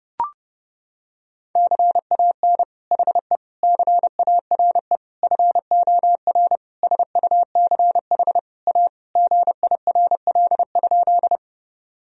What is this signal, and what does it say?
Morse code, heard as a keyed tone, about 30 wpm, 700 Hz. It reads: CAN HE CARE FOR SUCH A GIRL?
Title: Sentences from Top 1000 Words - Review Part 1 - 30wpm